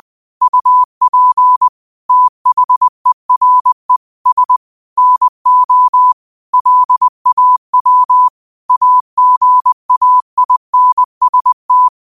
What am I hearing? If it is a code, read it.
Morse code, heard as a keyed tone, about 20 wpm, 1000 Hz. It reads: UP THERE S NO LAW AGAINST
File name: qotdfiles / 20wpm